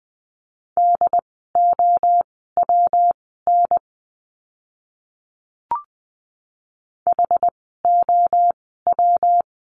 Morse code, heard as a keyed tone, about 20 wpm, spaced as for 14 wpm, 700 Hz. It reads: DOWN HOW
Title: 115 Lesson 19f - Letter w - Words 20wpm